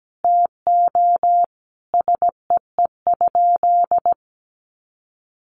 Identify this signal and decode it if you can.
Morse code, heard as a keyed tone, about 17 wpm, 700 Hz. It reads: TO SEE?
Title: Sentences from Top 1100 Words 17wpm